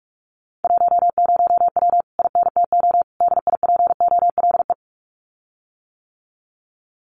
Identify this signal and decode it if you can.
Morse code, heard as a keyed tone, about 44 wpm, 700 Hz. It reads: 10W INTO DIPOLE
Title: Common QSO Phrases Encore 45wpm